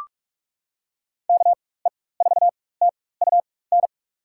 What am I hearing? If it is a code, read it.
Morse code, heard as a keyed tone, about 44 wpm, spaced as for 21 wpm, 700 Hz. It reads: KE4TUN
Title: USA Call Signs - 6 Characters - 4x Character Spacing - 45wpm